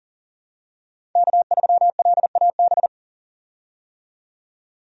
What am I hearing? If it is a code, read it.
Morse code, heard as a keyed tone, about 40 wpm, 700 Hz. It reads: K3LAB
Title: USA Call Signs - 5 Characters - 40wpm